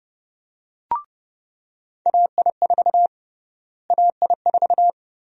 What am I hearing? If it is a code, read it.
Morse code, heard as a keyed tone, about 30 wpm, 700 Hz. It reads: AI4 AI4
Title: USA Call Sign Prefixes - Repeated 2x - 30wpm